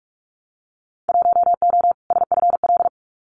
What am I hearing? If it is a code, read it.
Morse code, heard as a keyed tone, about 44 wpm, 700 Hz. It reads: 1O SFL
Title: Winter Field Day Exchanges 45wpm